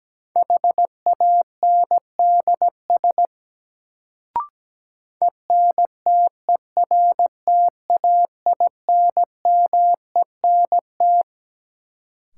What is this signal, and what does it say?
Morse code, heard as a keyed tone, about 17 wpm, 700 Hz. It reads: HANDS ENTERTAINMENT
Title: Sentences from Top 1000 Words - Review Part 6 - New Word Drill - 17wpm